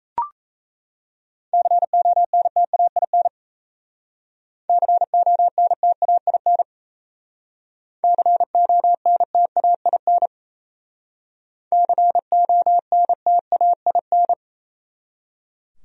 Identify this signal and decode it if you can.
Morse code, heard as a keyed tone, about 35 wpm, 700 Hz. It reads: CONTAIN CONTAIN CONTAIN CONTAIN
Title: Speed-Racing Top 500 Words 28wpm